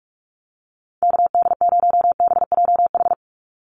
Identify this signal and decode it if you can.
Morse code, heard as a keyed tone, about 44 wpm, 700 Hz. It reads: KD0BJH
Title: USA Call Signs 45wpm